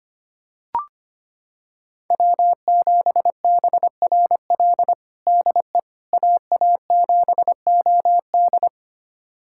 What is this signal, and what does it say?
Morse code, heard as a keyed tone, about 25 wpm, 700 Hz. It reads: W7BRL DE AA7OD
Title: USA Callsign DE Callsign - 5 Characters - 25wpm